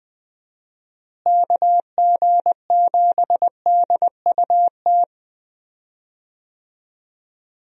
Morse code, heard as a keyed tone, about 20 wpm, 700 Hz. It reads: KG7DUT
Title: USA Call Signs - 6 Characters - Repeated 3x - 20wpm